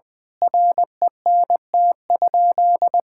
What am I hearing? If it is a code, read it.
Morse code, heard as a keyed tone, about 20 wpm, 700 Hz. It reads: RENT?
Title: Sentences from Top 1000 Words - Review Part 5 - 20wpm